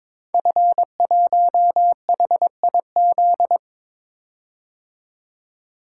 Morse code, heard as a keyed tone, about 22 wpm, 700 Hz. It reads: F1HIZ
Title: Code Groups - 5 Letter:Numbers - 22wpm